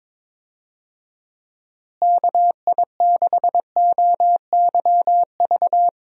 Morse code, heard as a keyed tone, about 22 wpm, 700 Hz. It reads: KI6OYV
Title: USA Call Signs - 6 Characters - Repeated 2x - 22wpm